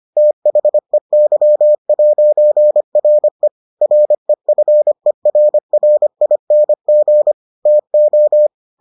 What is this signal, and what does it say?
Morse code, heard as a keyed tone, about 25 wpm, 600 Hz. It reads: THEY'RE REFERRING TO